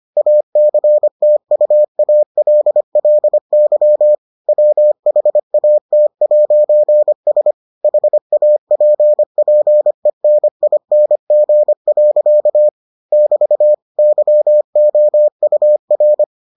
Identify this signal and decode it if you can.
Morse code, heard as a keyed tone, about 25 wpm, 600 Hz. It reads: ACTUALLY WHAT'S HAPPENING. <BT> YOUR